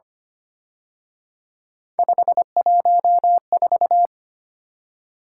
Morse code, heard as a keyed tone, about 31 wpm, 700 Hz. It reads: 514
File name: Speed-Racing - Three Letter-Number Combinations - 25wpm